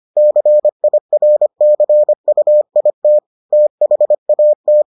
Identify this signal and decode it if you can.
Morse code, heard as a keyed tone, about 25 wpm, 600 Hz. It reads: CIRCUIT THAT